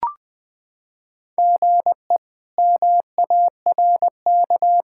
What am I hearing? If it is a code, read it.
Morse code, heard as a keyed tone, about 20 wpm, 700 Hz. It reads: GE MARK
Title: K1USN SST QSO Practice - 20wpm